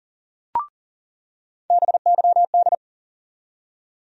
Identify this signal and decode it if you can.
Morse code, heard as a keyed tone, about 40 wpm, 700 Hz. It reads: BYD
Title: USA Call Sign Postfixes - 3 Letters 40wpm